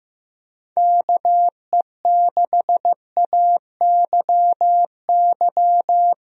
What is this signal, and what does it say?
Morse code, heard as a keyed tone, about 15 wpm, 700 Hz. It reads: KE6AYY
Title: USA Call Signs - 6 Characters - Repeated 2x - 15wpm